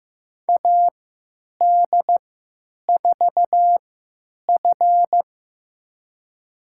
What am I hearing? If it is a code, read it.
Morse code, heard as a keyed tone, about 15 wpm, spaced as for 9 wpm, 700 Hz. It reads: AD4F
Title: USA Call Signs - 3x Character Spacing - 15wpm